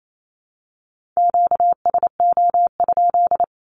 Morse code, heard as a keyed tone, about 28 wpm, 700 Hz. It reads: QSO?
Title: Q-Codes 28wpm